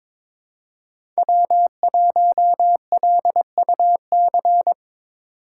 Morse code, heard as a keyed tone, about 22 wpm, 700 Hz. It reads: W1LUC
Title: USA Call Signs - 3x Character Spacing - 22wpm